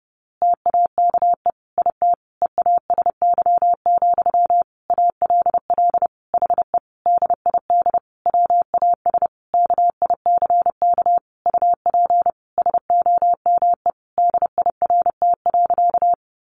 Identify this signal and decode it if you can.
Morse code, heard as a keyed tone, about 30 wpm, 700 Hz. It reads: TAKE IT EASY, ALL HE DID WAS KICK UP SOME DIRT.
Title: A Sound Of Thunder 30wpm